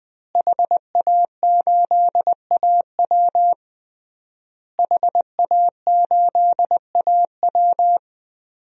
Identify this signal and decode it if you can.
Morse code, heard as a keyed tone, about 20 wpm, 700 Hz. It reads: HA8AW HA8AW
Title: DXCC-100 Call Signs - 5 Characters - Repeated 3x - 20wpm